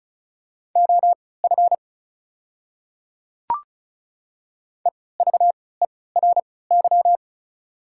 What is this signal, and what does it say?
Morse code, heard as a keyed tone, about 34 wpm, spaced as for 20 wpm, 690 Hz. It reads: OF EVERY
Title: 3x Character Spacing - Top 200 Words - 35wpm